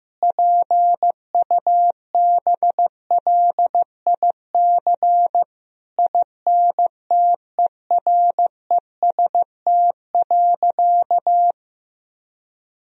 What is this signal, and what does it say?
Morse code, heard as a keyed tone, about 15 wpm, 700 Hz. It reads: PUBLIC INTEREST.
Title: Sentences from Top 1000 Words - Review Part 4 - 15wpm